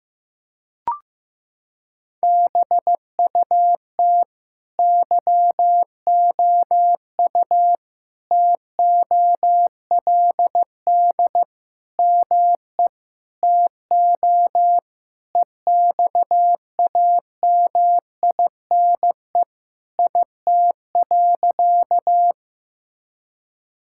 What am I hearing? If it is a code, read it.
Morse code, heard as a keyed tone, about 15 wpm, 700 Hz. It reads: BUT YOU TOLD ME TO EXAMINE IT.
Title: Sentences from Top 1600 Words 15wpm